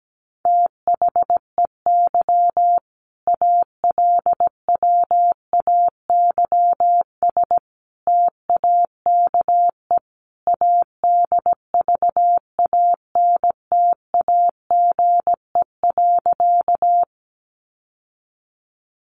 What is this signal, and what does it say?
Morse code, heard as a keyed tone, about 17 wpm, 700 Hz. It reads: THEY ALWAYS TAKE ADVANTAGE.
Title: Sentences from Top 900 Words 17wpm